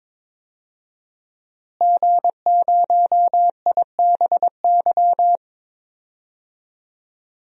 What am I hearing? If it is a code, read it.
Morse code, heard as a keyed tone, about 21 wpm, 700 Hz. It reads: G0IBY
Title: DXCC-100 Call Signs - 5 Characters - Repeated 2x - 22wpm